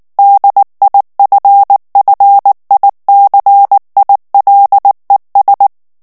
Morse code, heard as a keyed tone, about 19 wpm, 800 Hz. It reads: DIFFICILES
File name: FRANCAIS TEXTE CLAIR 2 VITESSE 18 MOTS MINUTES